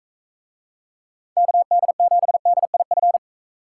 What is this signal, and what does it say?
Morse code, heard as a keyed tone, about 42 wpm, 700 Hz. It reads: KD7DIF